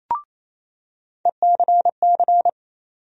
Morse code, heard as a keyed tone, about 28 wpm, 700 Hz. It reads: ECC
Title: USA Call Sign Postfixes - 3 Letters 28wpm